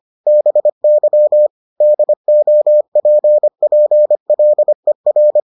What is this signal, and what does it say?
Morse code, heard as a keyed tone, about 25 wpm, 600 Hz. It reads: BY DOPPLER